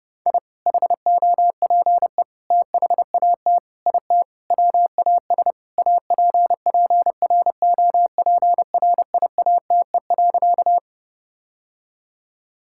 Morse code, heard as a keyed tone, about 30 wpm, 700 Hz. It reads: I HOPE THAT IT WAS APPROPRIATE.
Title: Sentences from Top 1000 Words - Review Part 5 - 30wpm